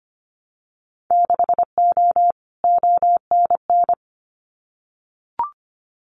Morse code, heard as a keyed tone, about 25 wpm, 700 Hz. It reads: 6O ONN
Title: Winter Field Day Exchanges Encore 25wpm